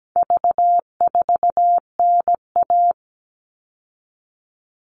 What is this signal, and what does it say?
Morse code, heard as a keyed tone, about 17 wpm, 700 Hz. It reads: V4NA
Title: All Cut Numbers - 4-Digit 17wpm